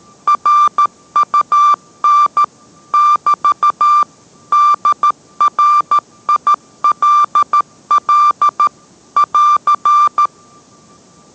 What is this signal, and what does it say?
Morse code, heard as a keyed tone, about 13 wpm, 1180 Hz. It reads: RUN <BT> DRILL <AR>